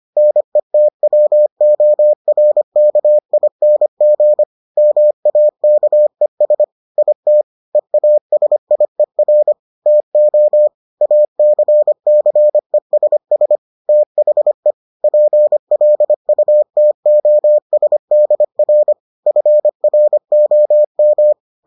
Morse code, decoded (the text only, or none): NETWORKING MAKES IT EASIER TO ACCESS THE PLUTOSDR FROM